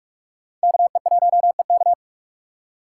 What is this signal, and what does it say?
Morse code, heard as a keyed tone, about 44 wpm, 700 Hz. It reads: KE1EK
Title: USA Call Signs - 5 Characters - 6x Character Spacing - 45wpm